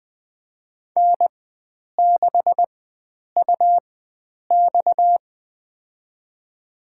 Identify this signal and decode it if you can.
Morse code, heard as a keyed tone, about 20 wpm, spaced as for 9 wpm, 700 Hz. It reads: N6UX
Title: USA Call Signs - 4 Characters - 4x Character Spacing - 20wpm